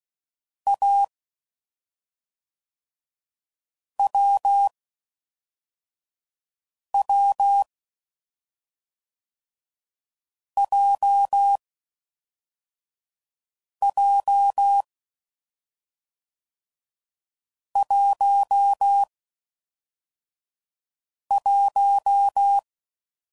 Morse code, heard as a keyed tone, about 16 wpm, 800 Hz. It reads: A W W J J 1 1